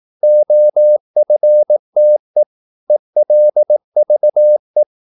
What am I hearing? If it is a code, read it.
Morse code, heard as a keyed tone, about 18 wpm, 600 Hz. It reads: OFTE ELVE
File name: Korte ord DK 18wpm